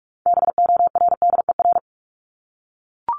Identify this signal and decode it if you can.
Morse code, heard as a keyed tone, about 44 wpm, 700 Hz. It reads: BORDER
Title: Top 1751-2000 Words 45wpm